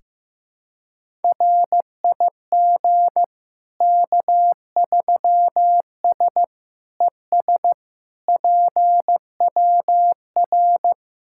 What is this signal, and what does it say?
Morse code, heard as a keyed tone, about 15 wpm, 700 Hz. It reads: RIG K3S ES PWR
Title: Practice QSO - Exchange 2 - Info - 15wpm